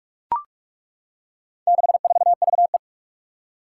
Morse code, heard as a keyed tone, about 41 wpm, 700 Hz. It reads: 64VE
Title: Four Challenging Letter-Number Combinations 45wpm